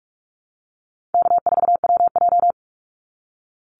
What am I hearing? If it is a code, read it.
Morse code, heard as a keyed tone, about 46 wpm, 700 Hz. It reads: K4WJ
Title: Easy Call Signs - 45wpm